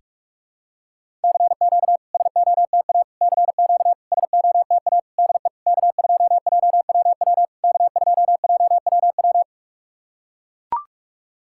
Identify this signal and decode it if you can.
Morse code, heard as a keyed tone, about 42 wpm, 690 Hz. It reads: CQ SOTA CQ SOTA DE K2JWW K2JWW
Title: SOTA QSO Practice 45wpm